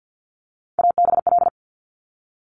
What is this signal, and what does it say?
Morse code, heard as a keyed tone, about 50 wpm, 700 Hz. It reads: ABL
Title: Top Three Letter Combinations in Words 50wpm